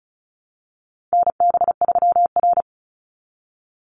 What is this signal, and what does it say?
Morse code, heard as a keyed tone, about 34 wpm, 700 Hz. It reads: NB3R
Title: Call Signs - 35wpm